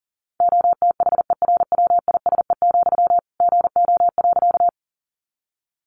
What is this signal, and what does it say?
Morse code, heard as a keyed tone, about 40 wpm, 700 Hz. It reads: OTHERWISE, GO.
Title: Sentences from Top 1200 Words 40wpm